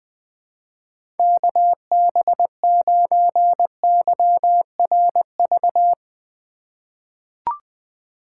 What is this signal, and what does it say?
Morse code, heard as a keyed tone, about 21 wpm, 700 Hz. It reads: KB9YRV
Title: USA Call Signs - 3x Character Spacing - 20wpm